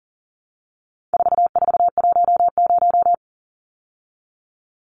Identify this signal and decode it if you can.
Morse code, heard as a keyed tone, about 40 wpm, 700 Hz. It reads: 4410
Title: Numbers - 4-Digit 40wpm